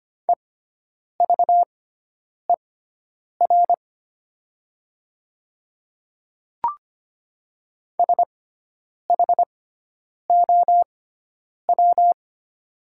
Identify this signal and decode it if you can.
Morse code, heard as a keyed tone, about 25 wpm, spaced as for 9 wpm, 700 Hz. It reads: EVER SHOW
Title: 6x Character Spacing - Top 500 Words - 25wpm